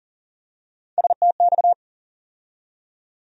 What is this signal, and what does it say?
Morse code, heard as a keyed tone, about 40 wpm, 700 Hz. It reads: STX
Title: USA Call Sign Postfixes - 3 Letters 40wpm